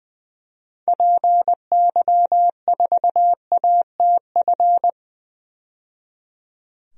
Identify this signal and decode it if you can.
Morse code, heard as a keyed tone, about 20 wpm, 700 Hz. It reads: PY4ATF
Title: DXCC-100 Call Signs - 6 Characters - 20wpm